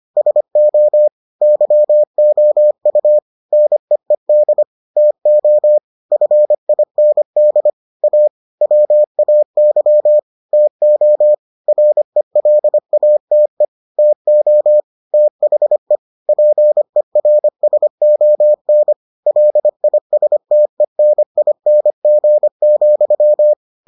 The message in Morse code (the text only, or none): SO YOU NEED TO FIND A WAY TO RELATE TO THE PERSON LISTENING,